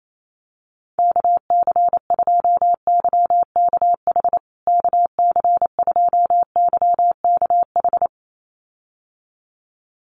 Morse code, heard as a keyed tone, about 28 wpm, 700 Hz. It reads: KC2YKH KC2YKH
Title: USA Call Signs Repeated 2x 28wpm